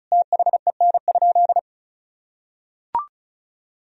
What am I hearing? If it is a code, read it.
Morse code, heard as a keyed tone, about 34 wpm, 700 Hz. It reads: THEN?
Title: Sentences from Top 100 Words and Abbreviations 35wpm